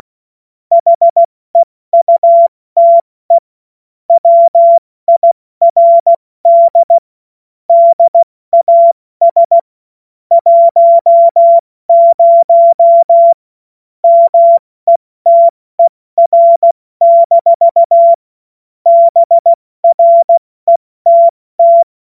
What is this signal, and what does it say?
Morse code, heard as a keyed tone, about 16 wpm, spaced as for 14 wpm, 680 Hz. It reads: HEUTE WIRD DAS 10 METER-BRETT
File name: Morsekurs / Witze / Witz 21. September 2021, 70 F80